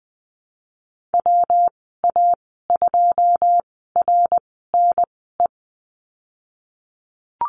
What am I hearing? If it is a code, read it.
Morse code, heard as a keyed tone, about 20 wpm, spaced as for 15 wpm, 700 Hz. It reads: WA2RNE
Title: 140 Lesson 23g - Letter f - Call Signs 20wpm